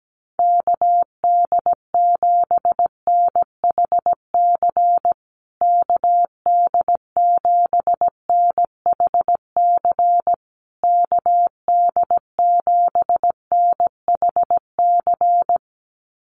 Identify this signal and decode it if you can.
Morse code, heard as a keyed tone, about 17 wpm, 700 Hz. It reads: KD7NHC KD7NHC KD7NHC
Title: USA Call Signs Repeated 3x 17wpm